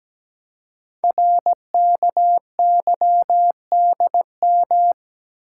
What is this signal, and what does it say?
Morse code, heard as a keyed tone, about 17 wpm, 700 Hz. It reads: RKYDM
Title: Code Groups - 5 Letters - 17wpm